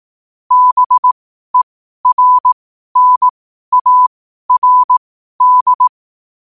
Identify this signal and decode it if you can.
Morse code, heard as a keyed tone, about 18 wpm, spaced as for 12 wpm, 1000 Hz. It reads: BERNARD
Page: New quotes every day in morse code at 12 Words per minute.